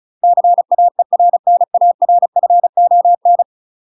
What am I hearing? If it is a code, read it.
Morse code, heard as a keyed tone, about 36 wpm, 700 Hz. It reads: CAERNARFON
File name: UK MASTER QSOs TRAINING / 35WPM / UK MASTER QSOs 0002